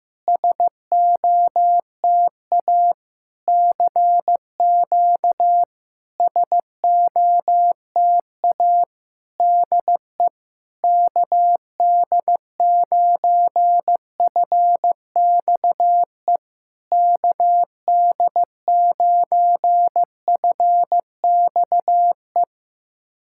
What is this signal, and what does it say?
Morse code, heard as a keyed tone, about 15 wpm, 700 Hz. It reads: SOTA CQ SOTA DE KD9FXE KD9FXE
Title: SOTA QSO Practice 15wpm